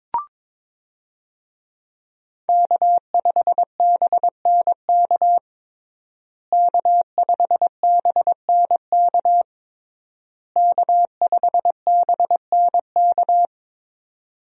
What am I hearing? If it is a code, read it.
Morse code, heard as a keyed tone, about 22 wpm, 700 Hz. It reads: K5BNK K5BNK K5BNK
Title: USA Call Signs - 5 Characters - Repeated 3x - 22wpm